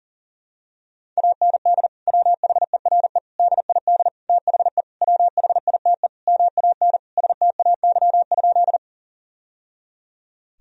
Morse code, heard as a keyed tone, about 40 wpm, 700 Hz. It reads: AND WHERE DID THE WHITE MAN STAY?
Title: Sentences from Top 1000 Words - Review Part 1 - 40wpm